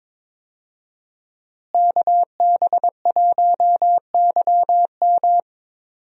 Morse code, heard as a keyed tone, about 21 wpm, 700 Hz. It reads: KB1YM
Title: USA Call Signs - 5 Characters - Repeated 2x - 22wpm